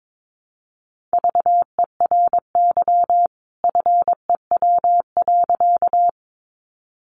Morse code, heard as a keyed tone, about 22 wpm, 700 Hz. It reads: VERY FEW.
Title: A Sound Of Thunder 22wpm